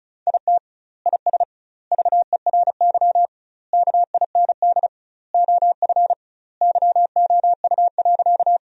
Morse code, heard as a keyed tone, about 34 wpm, 700 Hz. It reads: IT IS VERY KIND OF YOU.
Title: 2x Word-Spacing - Sentences from Top 200 Words - 35wpm